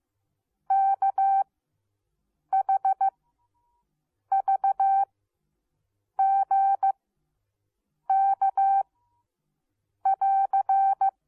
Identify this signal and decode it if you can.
Morse code, heard as a keyed tone, about 15 wpm, spaced as for 6 wpm, 780 Hz. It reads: KHVGK <AR>